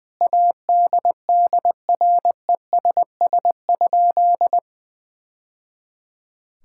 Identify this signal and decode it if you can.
Morse code, heard as a keyed tone, about 21 wpm, 700 Hz. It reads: ADDRESS?
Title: Sentences from Top 1000 Words - Review Part 6 - 20wpm